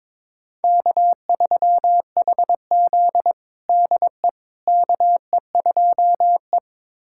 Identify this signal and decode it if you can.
Morse code, heard as a keyed tone, about 22 wpm, 700 Hz. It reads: K3HZ DE KE2E
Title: USA Callsign DE Callsign - 4 Characters - 22wpm